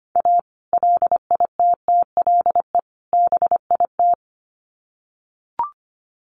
Morse code, heard as a keyed tone, about 25 wpm, 700 Hz. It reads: A LITTLE BIT
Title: Sets of 3 Words 25wpm